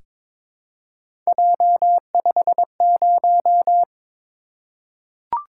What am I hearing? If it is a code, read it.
Morse code, heard as a keyed tone, about 22 wpm, 700 Hz. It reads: J50
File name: practice / 22wpm / Three Letter-Number Combinations - 22wpm